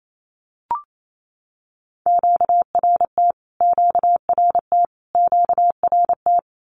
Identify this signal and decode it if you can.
Morse code, heard as a keyed tone, about 28 wpm, 700 Hz. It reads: QRT QRT QRT
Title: Q-Codes Repeated 3x 28wpm